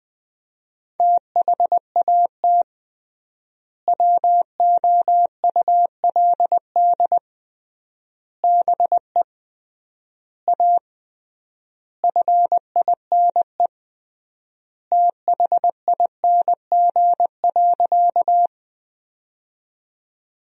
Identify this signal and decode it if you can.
Morse code, heard as a keyed tone, about 20 wpm, 700 Hz. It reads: THAT WOULD BE A FINE THING.
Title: 3x Word-Spacing - Sentences from Top 500 Words - 20wpm